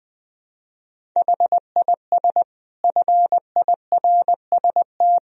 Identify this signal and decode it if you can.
Morse code, heard as a keyed tone, about 20 wpm, 700 Hz. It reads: HIS FIRST
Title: Sentences from Top 500 Words and Abbreviations 20wpm